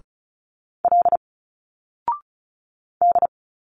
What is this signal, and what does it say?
Morse code, heard as a keyed tone, about 34 wpm, 700 Hz. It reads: L D
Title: Single Letters 35wpm